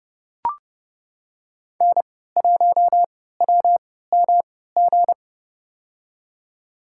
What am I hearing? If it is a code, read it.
Morse code, heard as a keyed tone, about 30 wpm, spaced as for 17 wpm, 700 Hz. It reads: N1WMG
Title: USA Call Signs - 5 Characters - 3x Character Spacing - 30wpm